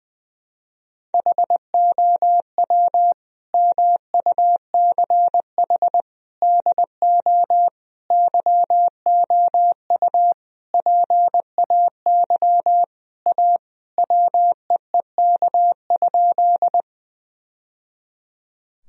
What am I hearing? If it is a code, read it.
Morse code, heard as a keyed tone, about 20 wpm, 700 Hz. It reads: HOW MUCH DO YOU PAY A WEEK?
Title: Sentences from Top 1000 Words - Review Part 2 - 20wpm